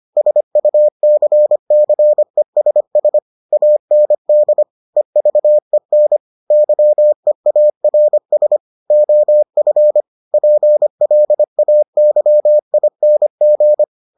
Morse code, heard as a keyed tone, about 25 wpm, 600 Hz. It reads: SUCCESS AND EVEN YEARS OF PLAYING